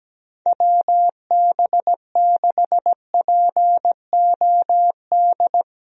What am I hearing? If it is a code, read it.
Morse code, heard as a keyed tone, about 17 wpm, 700 Hz. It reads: WB6POD
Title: USA Call Signs - 6 Characters - Repeated 3x - 17wpm